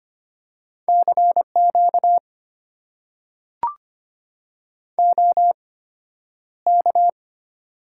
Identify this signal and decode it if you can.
Morse code, heard as a keyed tone, about 25 wpm, spaced as for 11 wpm, 690 Hz. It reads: CQ OK
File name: Top 90 QSO Elements - 8x Character Spacing - 25wpm